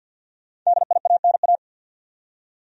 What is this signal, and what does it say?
Morse code, heard as a keyed tone, about 53 wpm, 700 Hz. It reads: DIANA
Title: USA Common Names - Max 5 Letters - 50wpm